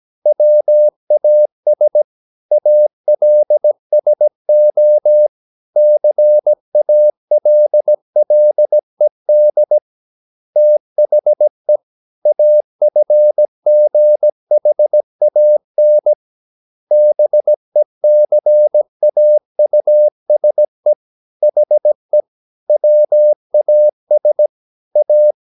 Morse code, heard as a keyed tone, about 17 wpm, 600 Hz. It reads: WAS ALSO CALLED THE AFGHAN BECAUSE HE WAS A
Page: Romanen Never af Ken Follet på engelsk.